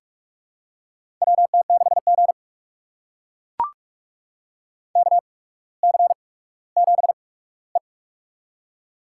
Morse code, heard as a keyed tone, about 44 wpm, spaced as for 19 wpm, 700 Hz. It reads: WT6G KC7E
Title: USA Call Signs - 4 Characters - 8x Character Spacing - 45wpm